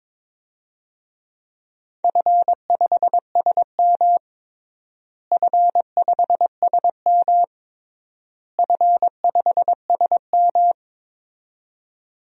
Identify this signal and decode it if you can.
Morse code, heard as a keyed tone, about 22 wpm, 700 Hz. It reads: F5SM F5SM F5SM
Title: DXCC-100 Call Signs - 4 Characters - Repeated 3x - 22wpm